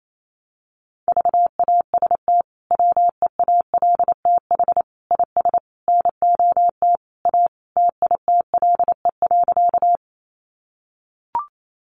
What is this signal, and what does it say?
Morse code, heard as a keyed tone, about 28 wpm, 700 Hz. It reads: VAST WEALTH IS NOT A TITLE.
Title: Sentences from Top 1300 Words 28wpm